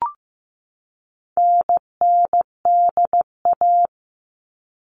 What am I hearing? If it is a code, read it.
Morse code, heard as a keyed tone, about 15 wpm, 700 Hz. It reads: NNDA
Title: All Cut Numbers - 4-Digit 15wpm